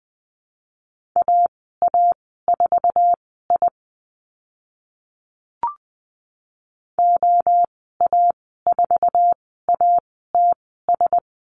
Morse code, heard as a keyed tone, about 20 wpm, spaced as for 14 wpm, 700 Hz. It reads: AA4I OA4ATS
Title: 082 Lesson 14g - Number 2 - Call Signs 20wpm